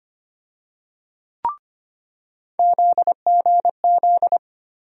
Morse code, heard as a keyed tone, about 25 wpm, 700 Hz. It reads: ZGZ
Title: USA Call Sign Postfixes - 3 Letters 25wpm